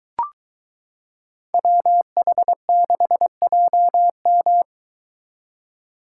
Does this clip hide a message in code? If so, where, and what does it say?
Yes: Morse code from 1.5 to 4.6 s, heard as a keyed tone, about 22 wpm, 700 Hz. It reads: WH6JM